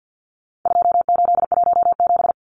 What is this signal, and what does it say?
Morse code, heard as a keyed tone, about 50 wpm, 700 Hz. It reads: 2817
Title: Numbers - 4-Digit 50wpm